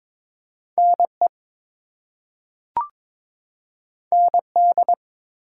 Morse code, heard as a keyed tone, about 22 wpm, 700 Hz. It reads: NE ND
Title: US States and Canadian Provinces 22wpm